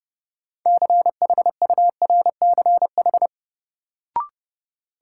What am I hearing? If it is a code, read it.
Morse code, heard as a keyed tone, about 30 wpm, 700 Hz. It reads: CHURCH
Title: Sets of 5 Words - New Word Drill - 30wpm